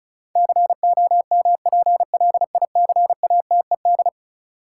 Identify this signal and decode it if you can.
Morse code, heard as a keyed tone, about 36 wpm, 700 Hz. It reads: COMPLICATED
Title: Sentences from Top 1200 Words - New Word Drill Alt - 35wpm